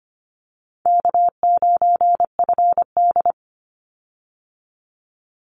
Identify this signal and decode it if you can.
Morse code, heard as a keyed tone, about 25 wpm, 700 Hz. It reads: K9FD
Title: Easy Call Signs Repeated 2x - 25wpm